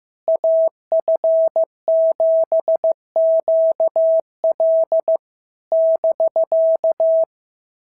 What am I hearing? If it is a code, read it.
Morse code, heard as a keyed tone, about 15 wpm, 650 Hz. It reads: AF7QL <BK>
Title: SOTA QSO Practice 15wpm